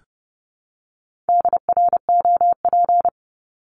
Morse code, heard as a keyed tone, about 30 wpm, 700 Hz. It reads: DROP
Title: Top 1251-1500 Words 30wpm